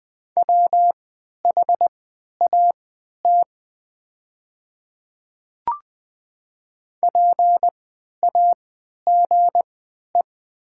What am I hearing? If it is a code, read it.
Morse code, heard as a keyed tone, about 20 wpm, spaced as for 11 wpm, 700 Hz. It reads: WHAT PAGE
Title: 3x Character Spacing - Top 300 Words - 20wpm